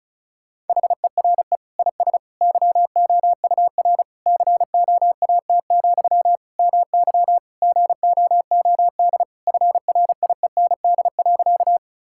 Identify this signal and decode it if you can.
Morse code, heard as a keyed tone, about 34 wpm, 700 Hz. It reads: HERE IS YOUR COAT, MY GOOD FRIEND.
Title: Sentences from Top 1600 Words 35wpm